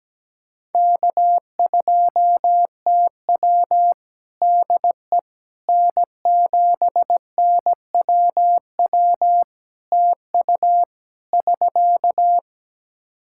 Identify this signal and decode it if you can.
Morse code, heard as a keyed tone, about 17 wpm, 700 Hz. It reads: K2TW DE N7NWW TU <SK>
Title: K1USN SST QSO Practice 17wpm